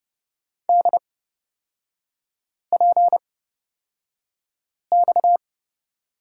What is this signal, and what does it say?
Morse code, heard as a keyed tone, about 30 wpm, spaced as for 5 wpm, 700 Hz. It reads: DPX
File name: Lesson 37e - Letter x - Three Characters 30wpm